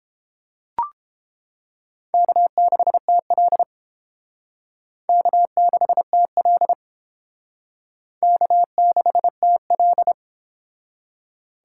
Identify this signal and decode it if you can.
Morse code, heard as a keyed tone, about 30 wpm, 700 Hz. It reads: K6TL K6TL K6TL